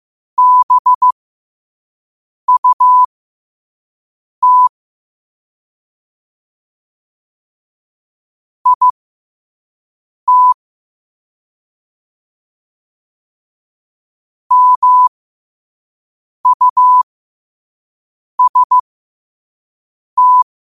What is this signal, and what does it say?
Morse code, heard as a keyed tone, about 15 wpm, spaced as for 5 wpm, 1000 Hz. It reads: BUT IT MUST
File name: qotdfiles / 5wpm